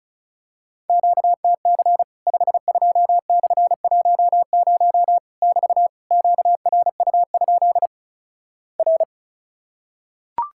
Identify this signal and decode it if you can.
Morse code, heard as a keyed tone, about 36 wpm, 700 Hz. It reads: QTC 52/10 <BT> QRU? R
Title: Worked All Europe DX Contest - QTC Exchange - 35wpm